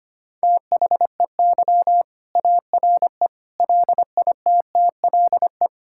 Morse code, heard as a keyed tone, about 25 wpm, 700 Hz. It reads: THEY ARE LITTLE
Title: Sentences from Top 1600 Words 25wpm